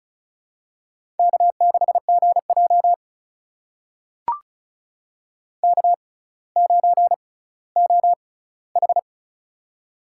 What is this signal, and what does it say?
Morse code, heard as a keyed tone, about 35 wpm, spaced as for 18 wpm, 700 Hz. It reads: K6GJ K9OH
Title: USA Call Signs - 4 Characters - 6x Character Spacing - 35wpm